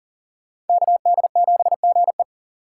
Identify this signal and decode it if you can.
Morse code, heard as a keyed tone, about 51 wpm, 700 Hz. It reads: KD7GE